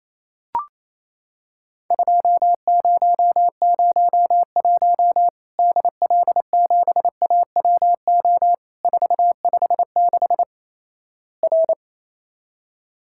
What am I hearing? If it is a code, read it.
Morse code, heard as a keyed tone, about 28 wpm, 700 Hz. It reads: 2001 DL7AWO 456 R
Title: Worked All Europe DX Contest - QTC Exchange - 28wpm